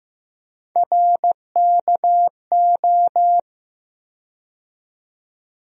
Morse code, heard as a keyed tone, about 15 wpm, 700 Hz. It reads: RKO
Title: USA Call Sign Postfixes - 3 Letters 15wpm